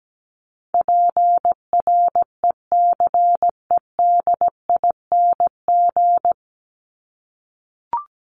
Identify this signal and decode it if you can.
Morse code, heard as a keyed tone, about 17 wpm, 700 Hz. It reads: PRECEDING
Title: ING Suffix - Top 100 Words with Base Words 17wpm